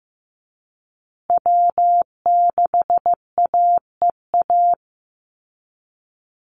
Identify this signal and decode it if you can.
Morse code, heard as a keyed tone, about 15 wpm, 700 Hz. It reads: W6AEA
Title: Call Signs Repeated 2x - 15wpm